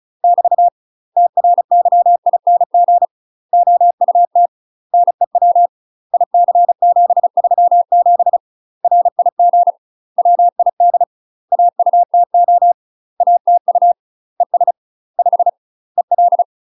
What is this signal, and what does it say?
Morse code, heard as a keyed tone, about 34 wpm, 700 Hz. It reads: <BT> TRYING OUT NEW IC737 RIG WID AUTO ATU ES 5 EL
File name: UK MASTER QSOs TRAINING / 35WPM / UK MASTER QSOs 0042